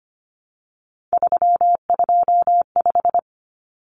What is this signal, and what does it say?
Morse code, heard as a keyed tone, about 25 wpm, 700 Hz. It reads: 325
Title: Common Cut Numbers - 3-Digit 25wpm